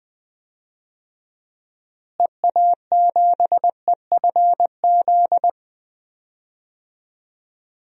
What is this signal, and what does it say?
Morse code, heard as a keyed tone, about 21 wpm, 700 Hz. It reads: EA7EFZ
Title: DXCC-100 Call Signs - 6 Characters - Repeated 2x - 20wpm